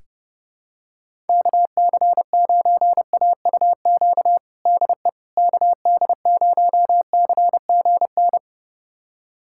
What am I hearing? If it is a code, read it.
Morse code, heard as a keyed tone, about 32 wpm, 700 Hz. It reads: KC9AUQ DE KD0CGN
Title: USA Callsign DE Callsign - 6 Characters - 30wpm